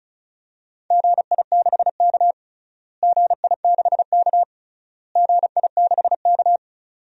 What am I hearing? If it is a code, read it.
Morse code, heard as a keyed tone, about 34 wpm, 700 Hz. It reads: GI6K GI6K GI6K
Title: DXCC-100 Call Signs - 4 Characters - Repeated 3x - 35wpm